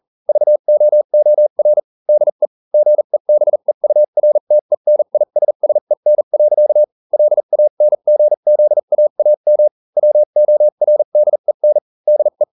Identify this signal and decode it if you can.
Morse code, heard as a keyed tone, about 40 wpm, spaced as for 35 wpm, 600 Hz. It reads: VOOR DE GEBEURTENISSEN. LANGZAAM WORDEN DE